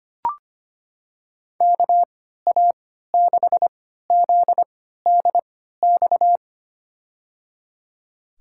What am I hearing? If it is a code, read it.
Morse code, heard as a keyed tone, about 25 wpm, spaced as for 14 wpm, 700 Hz. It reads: KA6ZDX
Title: USA Call Signs - 6 Characters - 3x Character Spacing - 25wpm